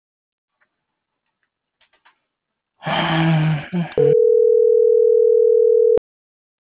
Ringback tone (North America), 440 + 480 Hz.